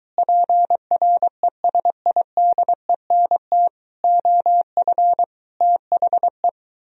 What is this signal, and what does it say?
Morse code, heard as a keyed tone, about 23 wpm, 700 Hz. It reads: PRESIDENT OF THE